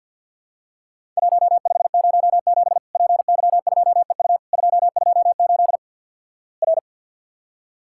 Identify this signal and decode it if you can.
Morse code, heard as a keyed tone, about 48 wpm, 700 Hz. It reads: 1507 PY2EU 228 R
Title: Worked All Europe DX Contest - QTC Exchange - 50wpm